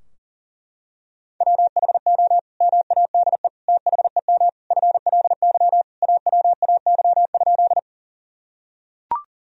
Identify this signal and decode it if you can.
Morse code, heard as a keyed tone, about 37 wpm, 700 Hz. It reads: WHO MADE THEM FLY AWAY?
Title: Sentences from Top 1000 Words - Review Part 2 - 40wpm